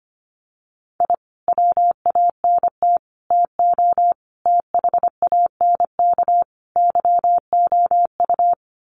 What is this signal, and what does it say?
Morse code, heard as a keyed tone, about 25 wpm, 700 Hz. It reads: I WANT TO THANK YOU
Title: Sets of 5 Words Encore 25wpm